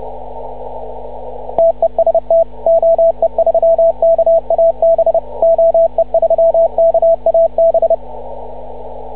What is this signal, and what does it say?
Morse code, heard as a keyed tone, about 29 wpm, 660 Hz, with radio noise. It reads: TEST OE3KAB OE3KAB